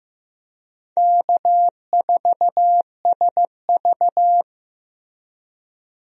Morse code, heard as a keyed tone, about 15 wpm, 700 Hz. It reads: K4SV
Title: USA Call Signs - 4 Characters - Repeated 2x - 15wpm